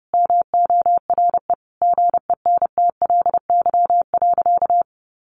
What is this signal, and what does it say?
Morse code, heard as a keyed tone, about 30 wpm, 700 Hz. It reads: MORE GENTLY.
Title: Sentences from Top 1500 Words 30wpm